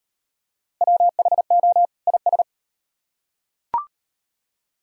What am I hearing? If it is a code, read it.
Morse code, heard as a keyed tone, about 33 wpm, 700 Hz. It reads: WHO IS